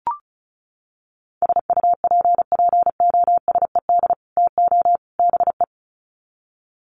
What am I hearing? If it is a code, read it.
Morse code, heard as a keyed tone, about 34 wpm, 700 Hz. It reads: SUPPOSED TO BE
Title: Sets of 3 Words 35wpm